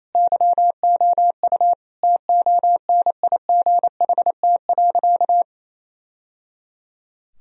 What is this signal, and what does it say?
Morse code, heard as a keyed tone, about 28 wpm, 700 Hz. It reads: YOU TONIGHT.
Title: Sentences from Top 1000 Words - Review Part 3 - 28wpm